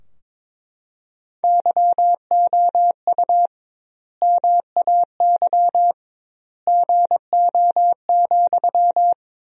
Morse code, heard as a keyed tone, about 22 wpm, 700 Hz. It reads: YOU MAY GO,
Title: 2x Word-Spacing - Sentences from Top 400 Words - 22wpm